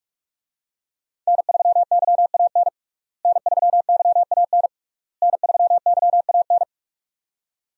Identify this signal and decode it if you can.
Morse code, heard as a keyed tone, about 46 wpm, 700 Hz. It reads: N3YAN N3YAN N3YAN
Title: USA Call Signs - 5 Characters - Repeated 3x - 45wpm